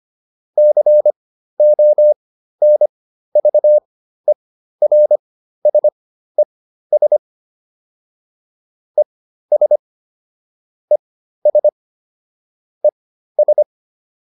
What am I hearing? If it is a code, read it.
Morse code, heard as a keyed tone, about 25 wpm, spaced as for 13 wpm, 600 Hz. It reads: CONVERSES ES ES ES